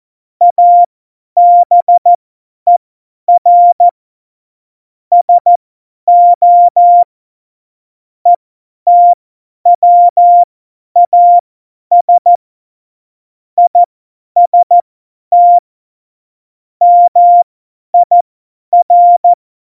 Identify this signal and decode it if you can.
Morse code, heard as a keyed tone, about 14 wpm, spaced as for 10 wpm, 700 Hz. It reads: ABER SO ETWAS IST MIR
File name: Morsekurs / Witze / Witz 29. Dezember 2020, 50 F70